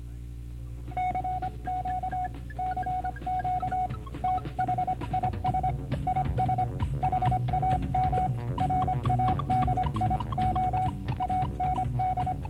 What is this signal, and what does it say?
Morse code, heard as a keyed tone, about 25 wpm, 740 Hz, with noise. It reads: CQ CQ THIS IS HAM RADIO AND